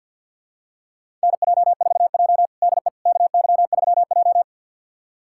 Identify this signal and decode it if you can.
Morse code, heard as a keyed tone, about 52 wpm, 700 Hz. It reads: NJ4J DE KY3J
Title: USA Callsign DE Callsign - 4 Characters - 50wpm